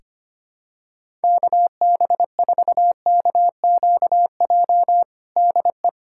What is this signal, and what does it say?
Morse code, heard as a keyed tone, about 25 wpm, 700 Hz. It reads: KB4KQJ DE
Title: USA Callsign DE Callsign 25wpm